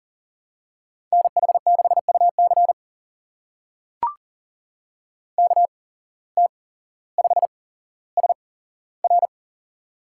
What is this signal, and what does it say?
Morse code, heard as a keyed tone, about 39 wpm, spaced as for 17 wpm, 700 Hz. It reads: NH6UC KT5SR
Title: USA Call Signs - 5 Characters - 8x Character Spacing - 40wpm